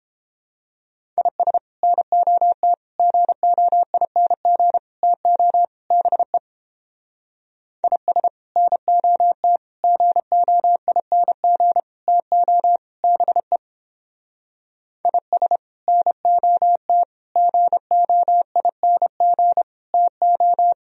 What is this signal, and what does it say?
Morse code, heard as a keyed tone, about 31 wpm, 700 Hz. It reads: IS NOT GOING TO BE IS NOT GOING TO BE IS NOT GOING TO